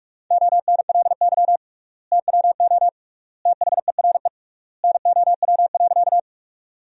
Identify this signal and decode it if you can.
Morse code, heard as a keyed tone, about 41 wpm, 700 Hz. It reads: ONLY TWO THERE NOW.
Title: 3x Word-Spacing - Sentences from Top 200 Words - 45wpm